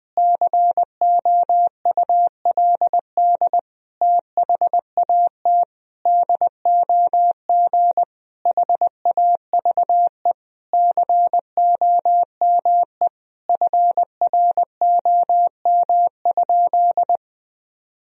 Morse code, heard as a keyed tone, about 20 wpm, 700 Hz. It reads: COULD THAT DOG HAVE COME FROM?
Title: Sentences from Top 1000 Words - Review Part 2 - 20wpm